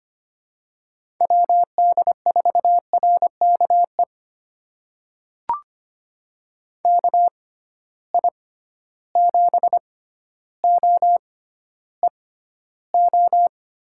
Morse code, heard as a keyed tone, about 25 wpm, spaced as for 13 wpm, 700 Hz. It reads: WD4RKE KI7OEO
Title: USA Call Signs - 6 Characters - 6x Character Spacing - 25wpm